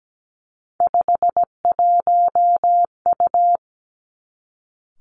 Morse code, heard as a keyed tone, about 17 wpm, 700 Hz. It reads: 51U
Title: Three Challenging Letter-Number Combinations 17wpm